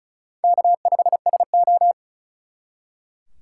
Morse code, heard as a keyed tone, about 45 wpm, 700 Hz. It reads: K5SO